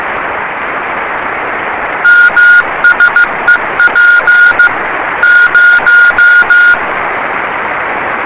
Morse code, heard as a keyed tone, about 15 wpm, 1430 Hz. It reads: MSEP 0
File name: rs30 cw